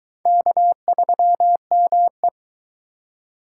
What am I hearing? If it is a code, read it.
Morse code, heard as a keyed tone, about 22 wpm, 700 Hz. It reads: K3ME